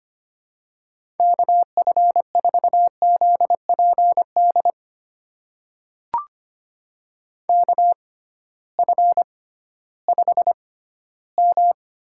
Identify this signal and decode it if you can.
Morse code, heard as a keyed tone, about 25 wpm, spaced as for 15 wpm, 700 Hz. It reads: KF4ZPD KF5M
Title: USA Call Signs - 6 Characters - 6x Character Spacing - 25wpm